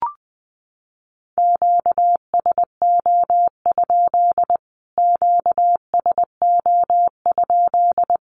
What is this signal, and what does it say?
Morse code, heard as a keyed tone, about 20 wpm, 700 Hz. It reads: QSO? QSO?
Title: Q-Codes Repeated 3x 20wpm